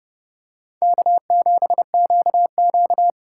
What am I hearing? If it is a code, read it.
Morse code, heard as a keyed tone, about 30 wpm, 700 Hz. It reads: K7QQ
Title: USA Call Signs - 4 Characters - 3x Character Spacing - 30wpm